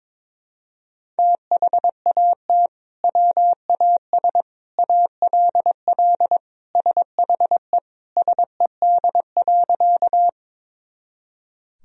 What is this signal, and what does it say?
Morse code, heard as a keyed tone, about 22 wpm, 700 Hz. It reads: THAT WAS ALL SHE SED.
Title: Sentences from Top 100 Words and Abbreviations 22wpm